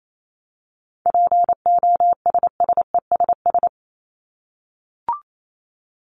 Morse code, heard as a keyed tone, about 28 wpm, 700 Hz. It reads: POSSESS
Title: Top 1751-2000 Words 28wpm